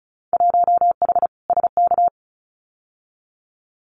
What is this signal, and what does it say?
Morse code, heard as a keyed tone, about 34 wpm, 700 Hz. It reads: 1H SK
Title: Winter Field Day Exchanges 35wpm